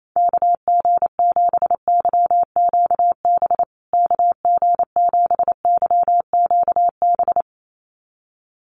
Morse code, heard as a keyed tone, about 28 wpm, 700 Hz. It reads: KG7YQB KG7YQB
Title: USA Call Signs Repeated 2x 28wpm